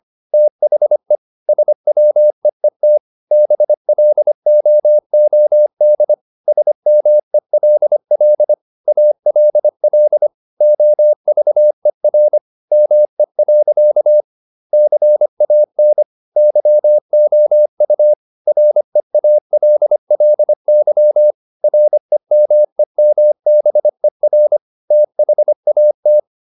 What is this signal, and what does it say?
Morse code, heard as a keyed tone, about 25 wpm, 600 Hz. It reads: THE SWEET BLOOD SMELL ALL OVER ME. CAN YOU REALLY REMEMBER THAT
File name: the old man and the sea part 1 25wpm